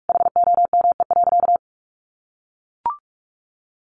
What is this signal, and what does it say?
Morse code, heard as a keyed tone, about 44 wpm, 700 Hz. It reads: HOME.
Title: Sentences from Top 800 Words 45wpm